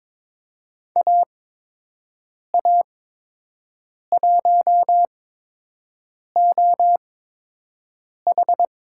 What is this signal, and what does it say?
Morse code, heard as a keyed tone, about 22 wpm, spaced as for 6 wpm, 700 Hz. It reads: AA1OH
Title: USA Call Signs - 8x Character Spacing - 22wpm